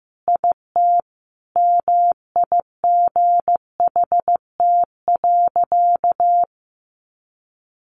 Morse code, heard as a keyed tone, about 15 wpm, 700 Hz. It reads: IT MIGHT.
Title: Sentences from Top 500 Words 15wpm